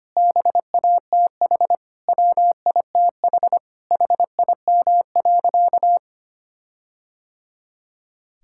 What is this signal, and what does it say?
Morse code, heard as a keyed tone, about 25 wpm, 700 Hz. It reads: BATH WITH HIM.
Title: Sentences from Top 1600 Words 25wpm